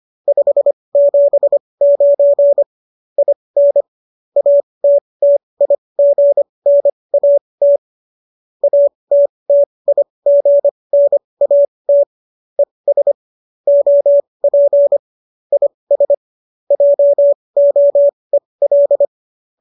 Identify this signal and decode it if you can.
Morse code, heard as a keyed tone, about 25 wpm, spaced as for 20 wpm, 570 Hz. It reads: 579 IN ATTIGNAT ATTIGNAT ES OP IS JOEL
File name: qso214 20